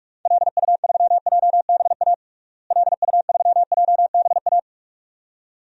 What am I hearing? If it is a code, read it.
Morse code, heard as a keyed tone, about 46 wpm, 700 Hz. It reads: LU3JBA LU3JBA
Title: DXCC-100 Call Signs - 6 Characters - Repeated 2x - 45wpm